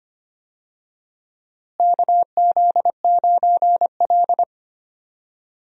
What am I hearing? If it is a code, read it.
Morse code, heard as a keyed tone, about 25 wpm, 700 Hz. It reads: KZ9L
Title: USA Call Signs - 4 Characters - Repeated 3x - 25wpm